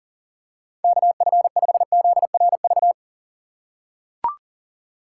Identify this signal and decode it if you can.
Morse code, heard as a keyed tone, about 40 wpm, 700 Hz. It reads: KF5ZRV
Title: USA Call Signs - 6 Characters - 4x Character Spacing - 40wpm